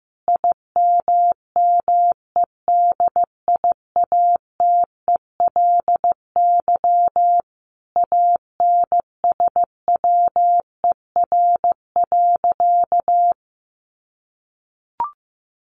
Morse code, heard as a keyed tone, about 15 wpm, 700 Hz. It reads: IMMEDIATELY ANSWER.
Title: Sentences from Top 1100 Words 15wpm